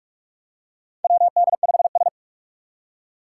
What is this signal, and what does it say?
Morse code, heard as a keyed tone, about 44 wpm, 700 Hz. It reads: WD5S
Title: Four Challenging Letter-Number Combinations 45wpm